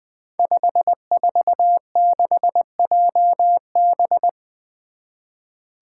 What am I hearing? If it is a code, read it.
Morse code, heard as a keyed tone, about 20 wpm, 700 Hz. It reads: 546JB
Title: Five Challenging Letter-Number Combinations 20wpm